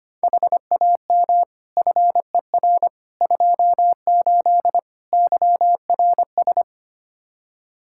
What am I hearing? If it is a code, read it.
Morse code, heard as a keyed tone, about 25 wpm, 700 Hz. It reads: HAM FER 28 YRS
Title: Practice QSO - Exchange 3 - More Info - 25wpm